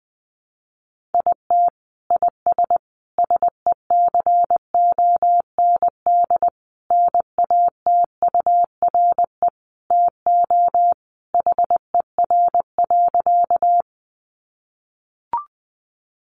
Morse code, heard as a keyed tone, about 20 wpm, 700 Hz. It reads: IT IS SECOND NATURE TO HER.
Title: Sentences from Top 600 Words 20wpm